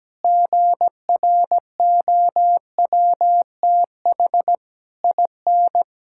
Morse code, heard as a keyed tone, about 17 wpm, 700 Hz. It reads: GROWTH IN
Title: Sentences from Top 1000 Words - Review Part 5 - 17wpm